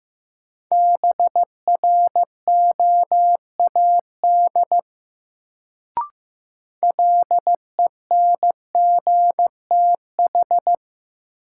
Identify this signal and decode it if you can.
Morse code, heard as a keyed tone, about 15 wpm, 700 Hz. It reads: BROAD LENGTH
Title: Sentences from Top 1000 Words - New Word Drill - 15wpm